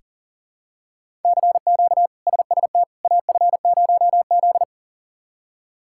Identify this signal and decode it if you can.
Morse code, heard as a keyed tone, about 40 wpm, 700 Hz. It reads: CQ SST AF0Z
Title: K1USN SST QSO Practice - 40wpm